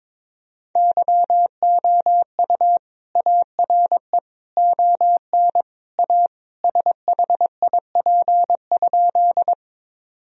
Morse code, heard as a keyed tone, about 22 wpm, 700 Hz. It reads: YOU ARE ON A SHIP?
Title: Sentences from Top 1000 Words - Review Part 2 - 22wpm